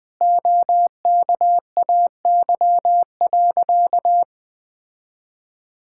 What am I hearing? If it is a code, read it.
Morse code, heard as a keyed tone, about 21 wpm, 700 Hz. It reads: OKAY.
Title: Sentences with 5-Word Phrases - 20wpm